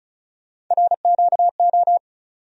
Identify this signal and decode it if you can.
Morse code, heard as a keyed tone, about 36 wpm, 700 Hz. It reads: RQO
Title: USA Call Sign Postfixes - 3 Letters 35wpm